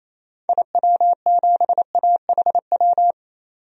Morse code, heard as a keyed tone, about 28 wpm, 700 Hz. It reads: IW7AHW
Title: DXCC-100 Call Signs - 6 Characters - 28wpm